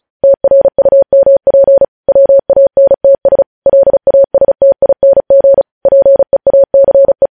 Morse code, heard as a keyed tone, about 36 wpm, 550 Hz. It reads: TRUMP WANTS LASTING PEACE
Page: News Headlines in Morse Code at 35 WPM